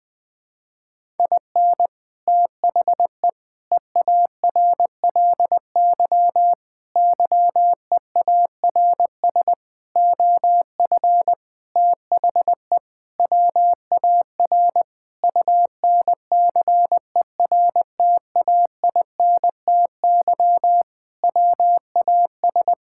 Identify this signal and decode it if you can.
Morse code, heard as a keyed tone, about 20 wpm, 700 Hz. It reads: IN THE EARLY YEARS OF THE WAR UNCERTAINTY WAS
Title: Sentences with 6-Word Phrases - Spoken Beginning - 20wpm